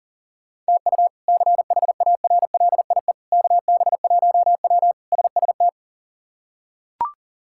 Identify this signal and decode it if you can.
Morse code, heard as a keyed tone, about 40 wpm, 700 Hz. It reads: TU CHARLIE KB1W SST
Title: K1USN SST QSO Practice - 40wpm